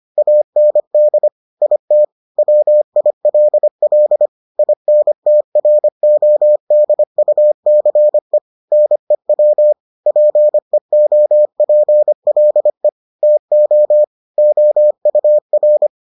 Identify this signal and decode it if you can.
Morse code, heard as a keyed tone, about 25 wpm, 600 Hz. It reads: AND IT WILL INTRODUCE NEW PEOPLE TO OUR